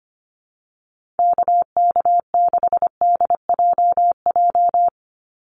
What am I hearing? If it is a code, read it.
Morse code, heard as a keyed tone, about 25 wpm, 700 Hz. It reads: KK6DJJ
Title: USA Call Signs 25wpm